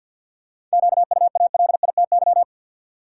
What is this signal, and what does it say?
Morse code, heard as a keyed tone, about 51 wpm, 700 Hz. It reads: QUALITY
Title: Seven Letter Words - 50wpm